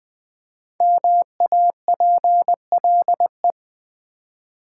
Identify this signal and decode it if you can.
Morse code, heard as a keyed tone, about 20 wpm, 700 Hz. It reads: MAPLE
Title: Easy Words with Hints - 20wpm